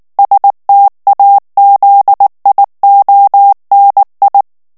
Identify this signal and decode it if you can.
Morse code, heard as a keyed tone, about 19 wpm, 800 Hz. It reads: STAZIONI
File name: morse mp3 / ITALIEN texte clair 2 vitesse 18 mots minute